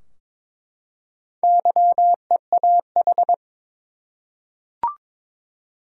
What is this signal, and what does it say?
Morse code, heard as a keyed tone, about 22 wpm, 700 Hz. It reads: YEAH
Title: Four Letter Words - 22wpm